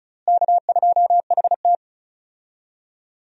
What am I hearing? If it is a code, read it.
Morse code, heard as a keyed tone, about 34 wpm, 700 Hz. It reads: K2HT
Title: USA Call Signs - 4 Characters - Repeated 3x - 35wpm